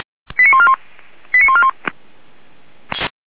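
Five-tone selective call (ZVEI), heard as quick runs of five tones, tones of about 70 ms. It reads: 89141 89141
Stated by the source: ZVEI five-tone call received by SDR, recorded by fire brigade Feuerwehr Orlamünde.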